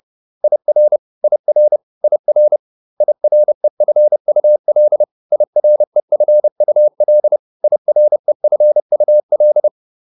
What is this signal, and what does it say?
Morse code, heard as a keyed tone, about 30 wpm, 600 Hz. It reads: IR IR IR IREFUL IREFUL IREFUL